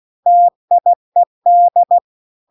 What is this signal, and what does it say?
Morse code, heard as a keyed tone, about 16 wpm, 700 Hz. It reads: TIED
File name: CW-The-Big-Twoheaded-River-2-16WPM